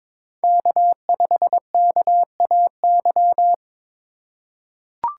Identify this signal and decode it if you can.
Morse code, heard as a keyed tone, about 23 wpm, 700 Hz. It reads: K5KAY
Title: USA Call Signs - 5 Characters - 3x Character Spacing - 22wpm